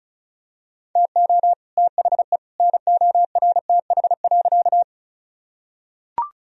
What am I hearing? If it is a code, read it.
Morse code, heard as a keyed tone, about 34 wpm, 700 Hz. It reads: TO THE NORTH.
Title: Sentences from Top 1000 Words - Review Part 1 - 35wpm